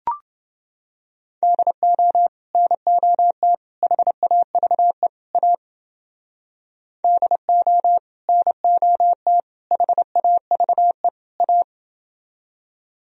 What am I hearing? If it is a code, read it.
Morse code, heard as a keyed tone, about 29 wpm, 700 Hz. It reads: DO NOT HAVE A DO NOT HAVE A